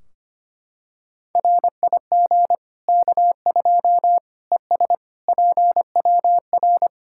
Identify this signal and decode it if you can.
Morse code, heard as a keyed tone, about 25 wpm, 700 Hz. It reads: RIG K2 ES PWR
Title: Practice QSO - Exchange 2 - Info - 25wpm